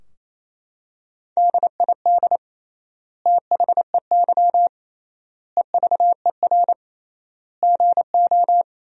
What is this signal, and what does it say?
Morse code, heard as a keyed tone, about 28 wpm, 700 Hz. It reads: DID THEY EVER GO
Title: 3x Word-Spacing - Sentences from Top 400 Words - 28wpm